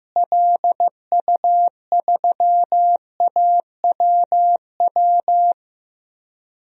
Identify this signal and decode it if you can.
Morse code, heard as a keyed tone, about 15 wpm, 700 Hz. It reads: LU3AWW
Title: DXCC-100 Call Signs - 6 Characters - Repeated 2x - 15wpm